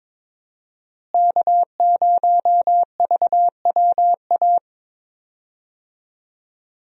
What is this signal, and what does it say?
Morse code, heard as a keyed tone, about 21 wpm, 700 Hz. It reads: K0VWA
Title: USA Call Signs - 5 Characters - Repeated 3x - 22wpm